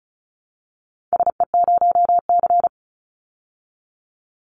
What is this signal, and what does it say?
Morse code, heard as a keyed tone, about 34 wpm, 700 Hz. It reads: SE0C
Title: Call Signs Repeated 2x - 35wpm